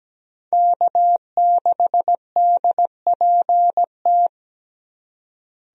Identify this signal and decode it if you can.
Morse code, heard as a keyed tone, about 17 wpm, 700 Hz. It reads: K6DPT
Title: USA Call Signs - 5 Characters - Repeated 2x - 17wpm